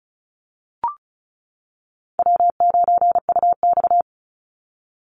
Morse code, heard as a keyed tone, about 34 wpm, 700 Hz. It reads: W9UX
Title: NA SOTA Chaser Callsigns 35wpm